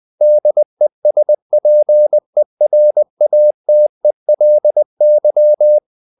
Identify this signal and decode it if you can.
Morse code, heard as a keyed tone, about 20 wpm, 600 Hz. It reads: DESPERATELY
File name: the old man and the sea part 2 20wpm